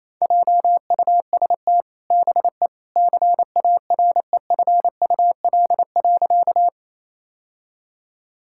Morse code, heard as a keyed tone, about 28 wpm, 700 Hz. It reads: JUST BE CAREFUL.
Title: Sentences from Top 1600 Words 28wpm